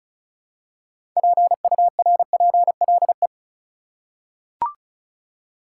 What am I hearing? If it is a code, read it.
Morse code, heard as a keyed tone, about 36 wpm, 700 Hz. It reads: PURPLE
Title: Six Letter Words - 35wpm